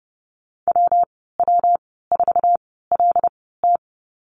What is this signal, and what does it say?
Morse code, heard as a keyed tone, about 30 wpm, spaced as for 17 wpm, 700 Hz. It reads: WW4LT
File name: Lesson 19g - Letter w - Call Signs 30wpm